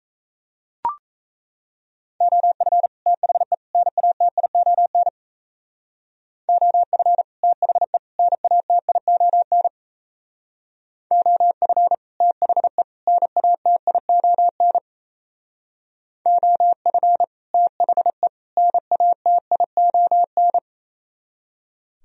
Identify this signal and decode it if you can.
Morse code, heard as a keyed tone, about 37 wpm, 700 Hz. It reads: OF THE NATION OF THE NATION OF THE NATION OF THE NATION